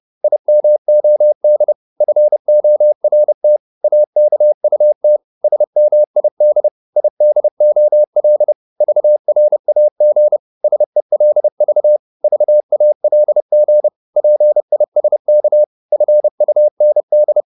Morse code, heard as a keyed tone, about 30 wpm, 600 Hz. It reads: IMOD FORT AKUT SMID IDOL VRAG SELV VALG PISK FUND
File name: Korte ord DK 30wpm